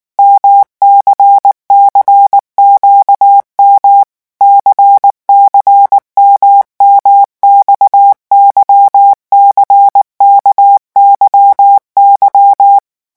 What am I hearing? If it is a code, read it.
Morse code, heard as a keyed tone, about 19 wpm, 800 Hz. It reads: MCCQM CCMMX YCKYY
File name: revision1-vitesse 18 mots